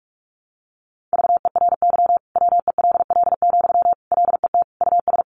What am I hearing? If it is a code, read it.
Morse code, heard as a keyed tone, about 44 wpm, 700 Hz. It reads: VERY WELL, LET US
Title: Sentences from Top 1200 Words 45wpm